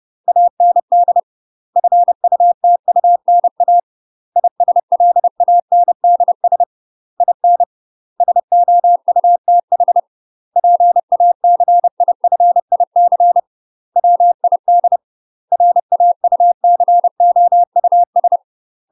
Morse code, heard as a keyed tone, about 30 wpm, 700 Hz. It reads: AND FUTUNA ISLANDS IN SOUTH PACIFIC WID RAUCOUS
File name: UK MASTER QSOs TRAINING / 30WPM / UK MASTER QSOs 0023